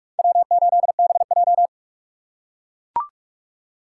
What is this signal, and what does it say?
Morse code, heard as a keyed tone, about 44 wpm, 700 Hz. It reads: W8BJ
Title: USA Call Signs - 4 Characters - 2x Character Spacing - 45wpm